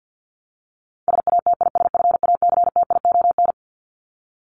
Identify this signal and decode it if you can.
Morse code, heard as a keyed tone, about 48 wpm, 700 Hz. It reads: SATISFACTION
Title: Top 1751-2000 Words 50wpm